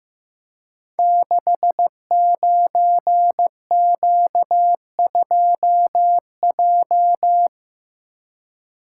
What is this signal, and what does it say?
Morse code, heard as a keyed tone, about 15 wpm, 700 Hz. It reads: 69Q2J
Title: Code Groups - 5 Letter:Numbers - 15wpm